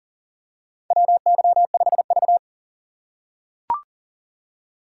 Morse code, heard as a keyed tone, about 41 wpm, 700 Hz. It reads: WY5V
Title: USA Call Signs - 4 Characters - 3x Character Spacing - 40wpm